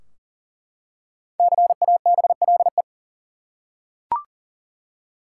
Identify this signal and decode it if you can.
Morse code, heard as a keyed tone, about 40 wpm, 700 Hz. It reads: CABLE
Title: Five Letter Words - 40wpm